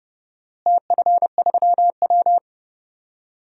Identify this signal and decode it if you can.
Morse code, heard as a keyed tone, about 32 wpm, 700 Hz. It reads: TF3W
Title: DXCC-100 Call Signs - 4 Characters - Repeated 3x - 30wpm